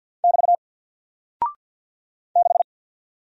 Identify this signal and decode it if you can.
Morse code, heard as a keyed tone, about 49 wpm, 700 Hz. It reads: <BT> 6
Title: Single Character (Letter:Number:Punctuation:Prosign) Alt 50wpm